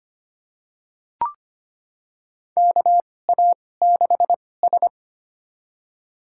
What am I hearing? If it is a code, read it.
Morse code, heard as a keyed tone, about 25 wpm, spaced as for 18 wpm, 700 Hz. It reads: KA6S
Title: USA Call Signs - 2x Character Spacing - 25wpm